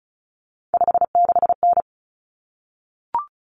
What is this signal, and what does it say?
Morse code, heard as a keyed tone, about 34 wpm, 700 Hz. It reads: 56N
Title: Common Cut Numbers - 3-Digit 35wpm